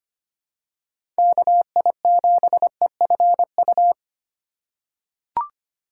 Morse code, heard as a keyed tone, about 25 wpm, 700 Hz. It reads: KI7EFU
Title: USA Call Signs - 6 Characters - 2x Character Spacing - 25wpm